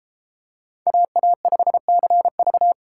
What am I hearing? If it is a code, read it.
Morse code, heard as a keyed tone, about 31 wpm, 700 Hz. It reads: AA5CV